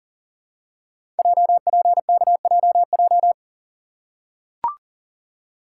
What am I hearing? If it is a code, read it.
Morse code, heard as a keyed tone, about 40 wpm, 700 Hz. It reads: JPKJJ
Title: Code Groups - 5 Letter:Numbers - 40wpm